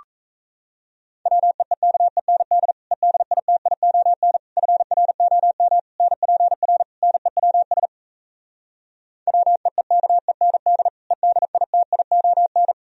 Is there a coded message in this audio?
Yes: Morse code from 1.3 to 12.7 s, heard as a keyed tone, about 37 wpm, 700 Hz. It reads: WEEKEND EDITION FROM NPR NEWS WEEKEND EDITION